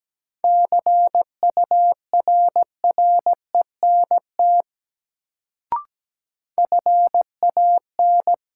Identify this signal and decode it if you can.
Morse code, heard as a keyed tone, about 17 wpm, 700 Hz. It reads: CURRENT FAN
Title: Sentences from Top 1000 Words - Review Part 4 - New Word Drill - 17wpm